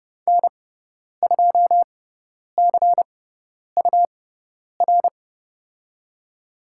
Morse code, heard as a keyed tone, about 30 wpm, spaced as for 10 wpm, 700 Hz. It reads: N2CUR
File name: Lesson 18g - Letter m - Call Signs 30wpm